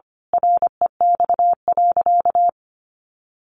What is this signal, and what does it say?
Morse code, heard as a keyed tone, about 25 wpm, 700 Hz. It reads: REX.
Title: A Sound Of Thunder 25wpm